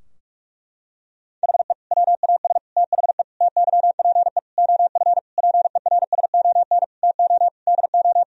Morse code, heard as a keyed tone, about 47 wpm, 700 Hz. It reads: HE WAS THE TYPE OF PERSON TO DO
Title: Sentences from Top 1000 Words - Review Part 3 - 45wpm